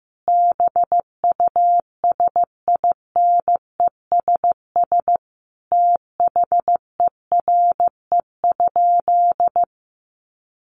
Morse code, heard as a keyed tone, about 15 wpm, 700 Hz. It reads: BUSINESS THERE?
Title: Sentences from Top 600 Words 15wpm